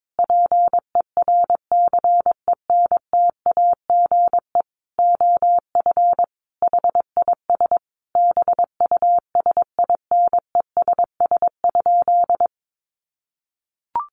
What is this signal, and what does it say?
Morse code, heard as a keyed tone, about 22 wpm, 700 Hz. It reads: PERCENTAGE OF HIS BUSINESS?
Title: Sentences from Top 1300 Words 22wpm